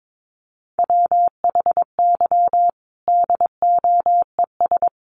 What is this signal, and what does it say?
Morse code, heard as a keyed tone, about 22 wpm, 700 Hz. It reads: WHY DOES
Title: Sentences from Top 1300 Words 22wpm